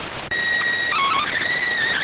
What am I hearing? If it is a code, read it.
SSTV (slow-scan television), PD 290.